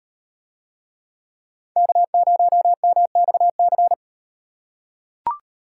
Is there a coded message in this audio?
Yes: Morse code from 1.8 to 3.9 s, heard as a keyed tone, about 36 wpm, 700 Hz. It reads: K0MXC